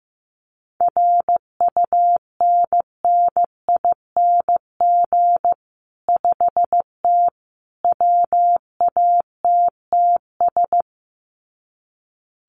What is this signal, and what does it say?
Morse code, heard as a keyed tone, about 15 wpm, 700 Hz. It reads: RUNNING 5T WATTS
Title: Common QSO Phrases 15wpm